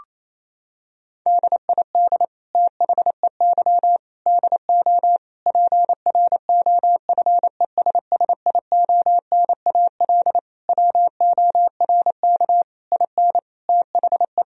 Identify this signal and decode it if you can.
Morse code, heard as a keyed tone, about 28 wpm, 700 Hz. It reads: DID THEY DO PROFESSIONAL WORK IN THE
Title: Sentences from Top 1000 Words - Review Part 4 - 28wpm